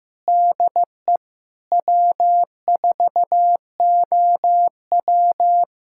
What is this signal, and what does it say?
Morse code, heard as a keyed tone, about 15 wpm, 700 Hz. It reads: DE W4OW
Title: USA Callsign DE Callsign - 4 Characters - 15wpm